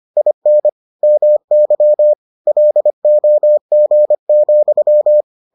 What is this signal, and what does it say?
Morse code, heard as a keyed tone, about 25 wpm, 600 Hz. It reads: IN MY LOG,